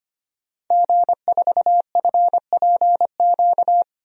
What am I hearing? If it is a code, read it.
Morse code, heard as a keyed tone, about 25 wpm, 700 Hz. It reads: G4FPQ
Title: DXCC-100 Call Signs - 5 Characters - Repeated 3x - 25wpm